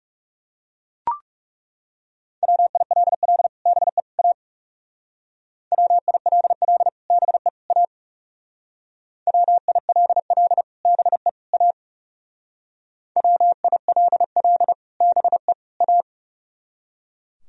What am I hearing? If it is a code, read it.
Morse code, heard as a keyed tone, about 37 wpm, 700 Hz. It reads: WILL BE A WILL BE A WILL BE A WILL BE A
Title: Speed-Racing - Sets of 3 Words 30wpm